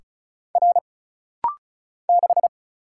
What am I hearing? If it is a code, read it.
Morse code, heard as a keyed tone, about 35 wpm, 700 Hz. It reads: R 6
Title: Single Letter-Number - Rapid-Fire 35wpm